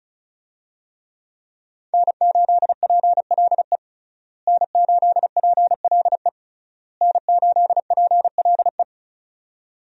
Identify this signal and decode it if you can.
Morse code, heard as a keyed tone, about 36 wpm, 700 Hz. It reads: N8PLE N8PLE N8PLE
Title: USA Call Signs - 5 Characters - Repeated 3x - 35wpm